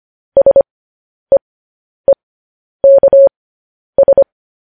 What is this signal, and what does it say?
Morse code, heard as a keyed tone, about 25 wpm, spaced as for 10 wpm, 550 Hz. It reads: SEEKS